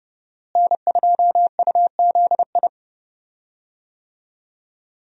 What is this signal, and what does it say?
Morse code, heard as a keyed tone, about 31 wpm, 700 Hz. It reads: N2UZI
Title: USA Call Signs - 5 Characters - Repeated 2x - 30wpm